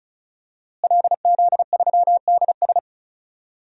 Morse code, heard as a keyed tone, about 36 wpm, 700 Hz. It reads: LZ3DS
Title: DXCC-100 Call Signs - 5 Characters - Repeated 2x - 35wpm